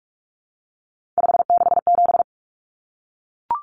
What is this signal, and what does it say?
Morse code, heard as a keyed tone, about 44 wpm, 700 Hz. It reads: 567
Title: Numbers - 3-Digit 45wpm